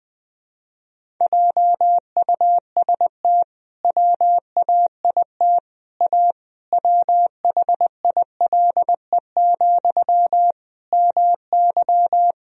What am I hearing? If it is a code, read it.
Morse code, heard as a keyed tone, about 20 wpm, 700 Hz. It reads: JUST WAIT A WHILE, MY
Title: Sentences from Top 1000 Words - Review Part 1 - 20wpm